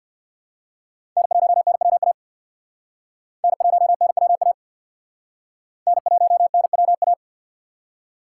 Morse code, heard as a keyed tone, about 55 wpm, 700 Hz. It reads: N1NWA N1NWA N1NWA
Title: Speed Racing - USA Call Signs 50wpm